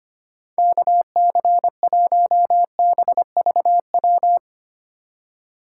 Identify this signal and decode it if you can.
Morse code, heard as a keyed tone, about 25 wpm, 700 Hz. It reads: KC1BVW
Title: USA Call Signs - 6 Characters - 4x Character Spacing - 25wpm